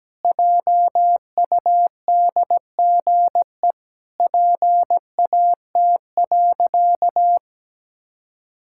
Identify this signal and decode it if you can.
Morse code, heard as a keyed tone, about 17 wpm, 700 Hz. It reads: JUDGE PAT.
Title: Pangram Challenge v2 - 17wpm